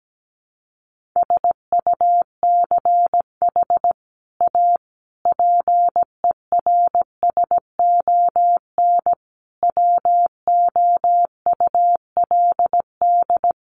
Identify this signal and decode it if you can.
Morse code, heard as a keyed tone, about 17 wpm, 700 Hz. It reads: SUCH A PERSON WOULD
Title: Sentences from Top 1400 Words 17wpm